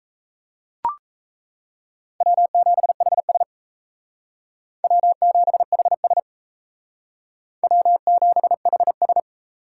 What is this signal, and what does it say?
Morse code, heard as a keyed tone, about 33 wpm, 700 Hz. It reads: W7HS W7HS W7HS
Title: Speed Racing - USA 1x2 Call Signs 28wpm